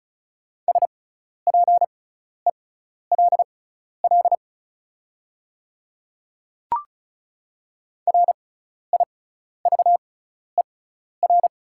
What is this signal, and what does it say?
Morse code, heard as a keyed tone, about 36 wpm, spaced as for 12 wpm, 700 Hz. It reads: SPELL RIVER
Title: 6x Character Spacing - Top 300 Words - 35wpm